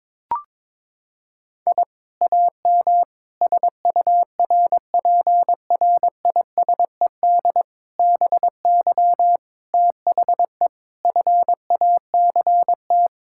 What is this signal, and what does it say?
Morse code, heard as a keyed tone, about 22 wpm, 700 Hz. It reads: I AM SURPRISED BY THE FACT
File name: practice / 22wpm / Sentences with 4-Word Phrases - 22wpm